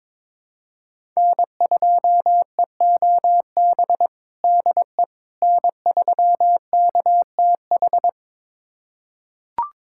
Morse code, heard as a keyed tone, about 21 wpm, 700 Hz. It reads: N2EOB DE N3KTH
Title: USA Callsign DE Callsign - 5 Characters - 22wpm